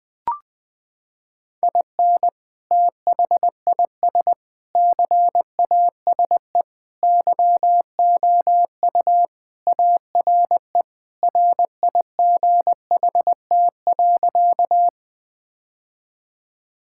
Morse code, heard as a keyed tone, about 20 wpm, 700 Hz. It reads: IN THIS CASE YOU ARE RIGHT.
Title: Sentences with 3-Word Phrases - 20wpm